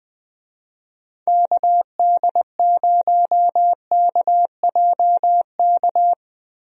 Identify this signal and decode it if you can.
Morse code, heard as a keyed tone, about 21 wpm, 700 Hz. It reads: KD0KJK
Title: USA Call Signs - 6 Characters - Repeated 3x - 20wpm